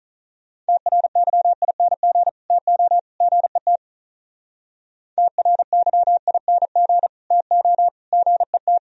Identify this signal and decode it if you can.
Morse code, heard as a keyed tone, about 43 wpm, 700 Hz. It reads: TRYING TO GET TRYING TO GET
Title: Speed-Racing - Sets of 3 Words 35wpm